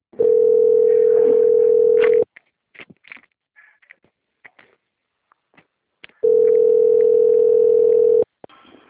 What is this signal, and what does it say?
Ringback tone (North America), 440 + 480 Hz.
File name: out- - - - - .172